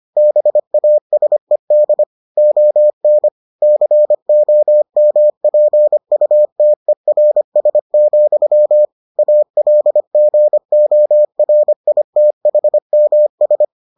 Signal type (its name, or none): Morse code